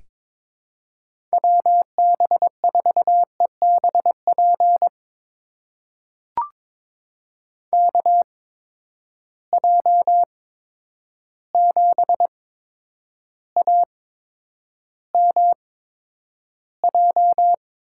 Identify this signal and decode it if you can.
Morse code, heard as a keyed tone, about 22 wpm, spaced as for 9 wpm, 700 Hz. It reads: WB4EBP KJ7AMJ
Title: USA Call Signs - 6 Characters - 8x Character Spacing - 22wpm